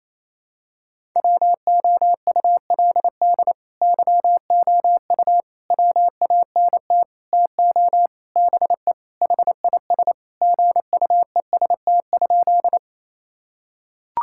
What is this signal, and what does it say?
Morse code, heard as a keyed tone, about 28 wpm, 700 Hz. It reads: WOULD YOU WANT TO BE HIS GUEST?
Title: Sentences from Top 1600 Words 28wpm